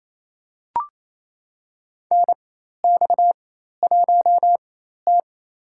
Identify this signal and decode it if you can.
Morse code, heard as a keyed tone, about 28 wpm, spaced as for 13 wpm, 700 Hz. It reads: NX1T
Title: USA Call Signs - 4 Characters - 4x Character Spacing - 28wpm